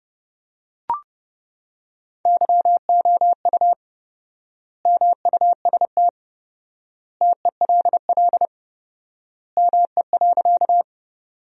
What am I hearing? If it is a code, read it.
Morse code, heard as a keyed tone, about 30 wpm, 700 Hz. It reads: YOU MUST TELL ME.
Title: 4x Word-Spacing - Sentences from Top 200 Words - 30wpm